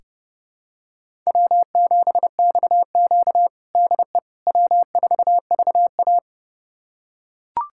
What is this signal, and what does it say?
Morse code, heard as a keyed tone, about 30 wpm, 700 Hz. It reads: W7XQ DE W4VA
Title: USA Callsign DE Callsign - 4 Characters - 30wpm